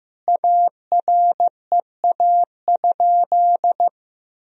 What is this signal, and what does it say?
Morse code, heard as a keyed tone, about 15 wpm, 700 Hz. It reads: AREA?
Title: Sentences from Top 1000 Words - Review Part 2 - 15wpm